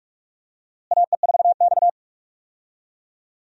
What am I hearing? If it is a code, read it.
Morse code, heard as a keyed tone, about 46 wpm, 700 Hz. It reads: AE4X
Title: USA Call Signs - 4 Characters - Repeated 3x - 45wpm